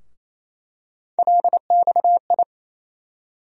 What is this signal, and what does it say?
Morse code, heard as a keyed tone, about 28 wpm, 700 Hz. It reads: LXI
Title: USA Call Sign Postfixes - 3 Letters 28wpm